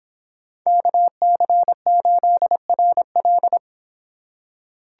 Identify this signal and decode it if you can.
Morse code, heard as a keyed tone, about 26 wpm, 700 Hz. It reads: KC8RL